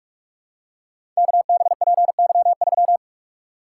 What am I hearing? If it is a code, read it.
Morse code, heard as a keyed tone, about 44 wpm, 700 Hz. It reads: KBPY3
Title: Code Groups - 5 Letter:Numbers - 45wpm